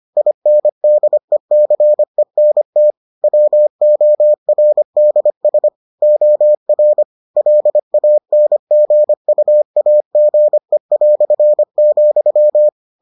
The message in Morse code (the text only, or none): INDECENT WORDS OR LANGUAGE",